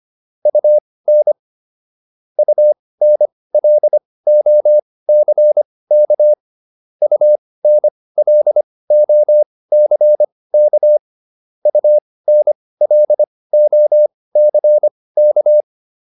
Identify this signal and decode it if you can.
Morse code, heard as a keyed tone, about 25 wpm, spaced as for 18 wpm, 600 Hz. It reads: UN UNLOCK UNLOCK UNLOCK